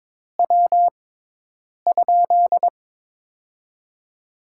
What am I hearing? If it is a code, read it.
Morse code, heard as a keyed tone, about 22 wpm, spaced as for 8 wpm, 700 Hz. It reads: W?
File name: Top 20 QSO Elements - 6x Character Spacing - 22wpm